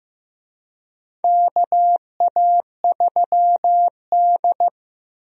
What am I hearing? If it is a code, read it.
Morse code, heard as a keyed tone, about 15 wpm, 700 Hz. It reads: KA3D
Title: USA Call Signs - 4 Characters - Repeated 2x - 15wpm